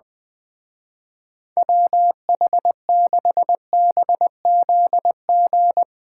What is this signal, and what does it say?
Morse code, heard as a keyed tone, about 23 wpm, 700 Hz. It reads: WH6BZG
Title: Speed Racing - USA Call Signs 20wpm